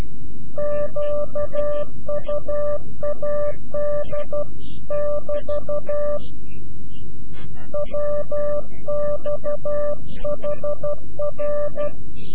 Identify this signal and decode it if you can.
Morse code, heard as a keyed tone, about 12 wpm, 590 Hz. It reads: QUAD <BT> WX HR
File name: CW-QSO / CQ Call / CQ-QSO